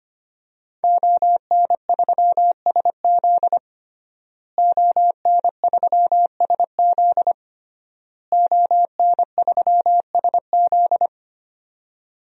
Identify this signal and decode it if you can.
Morse code, heard as a keyed tone, about 25 wpm, 700 Hz. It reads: ON3SZ ON3SZ ON3SZ
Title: DXCC-100 Call Signs - 5 Characters - Repeated 3x - 25wpm